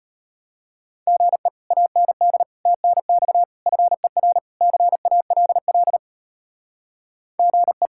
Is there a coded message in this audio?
Yes: Morse code from 1.1 to 7.9 s, heard as a keyed tone, about 33 wpm, 700 Hz. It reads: GE AND TNX FER CALL GE